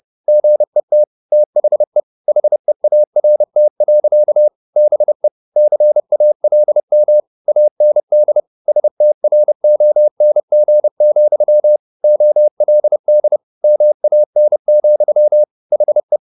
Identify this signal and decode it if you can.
Morse code, heard as a keyed tone, about 30 wpm, 600 Hz. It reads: GET THE HEART. BE CALM AND STRONG, OLD MAN, HE